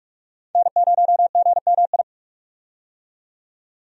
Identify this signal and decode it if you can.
Morse code, heard as a keyed tone, about 47 wpm, 700 Hz. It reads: N0GMI
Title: USA Call Signs - 5 Characters - 45wpm